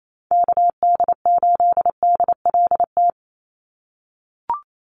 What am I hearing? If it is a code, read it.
Morse code, heard as a keyed tone, about 28 wpm, 700 Hz. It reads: KD8DLT
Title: USA Call Signs 28wpm